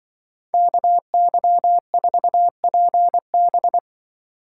Morse code, heard as a keyed tone, about 24 wpm, 700 Hz. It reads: KY4PB
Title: Speed Racing - USA 2x2 Call Signs 20wpm